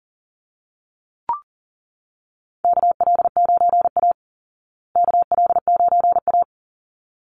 Morse code, heard as a keyed tone, about 40 wpm, 700 Hz. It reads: KL9A KL9A
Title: Call Signs Repeated 3x - 40wpm